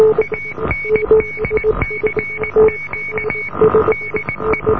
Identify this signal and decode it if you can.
Morse code, heard as a keyed tone, about 20 wpm, 2250 Hz, with noise. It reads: AND CQ DE